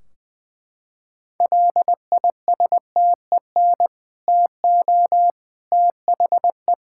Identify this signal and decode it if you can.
Morse code, heard as a keyed tone, about 21 wpm, 700 Hz. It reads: LISTEN TO THE
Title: Sentences from Top 1600 Words 20wpm